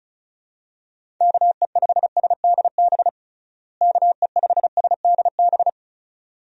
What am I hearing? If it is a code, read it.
Morse code, heard as a keyed tone, about 34 wpm, 700 Hz. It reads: KE5SDB KE5SDB
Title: USA Call Signs - 6 Characters - Repeated 2x - 35wpm